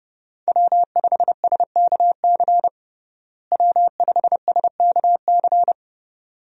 Morse code, heard as a keyed tone, about 32 wpm, 700 Hz. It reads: W5SKC W5SKC
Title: USA Call Signs - 5 Characters - Repeated 2x - 30wpm